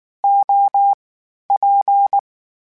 Morse code, heard as a keyed tone, about 19 wpm, 800 Hz. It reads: O P
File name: Variable Speed / Variable Speed-mp3s / TO sp @18